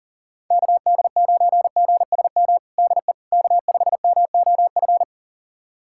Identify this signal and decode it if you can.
Morse code, heard as a keyed tone, about 40 wpm, 700 Hz. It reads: KD9GSM DE K5MOF
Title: USA Callsign DE Callsign 40wpm